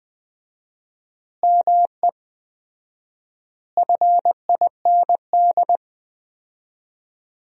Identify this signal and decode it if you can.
Morse code, heard as a keyed tone, about 20 wpm, 700 Hz. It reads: ME FIND
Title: 4x Word-Spacing - Sentences from Top 200 Words - 20wpm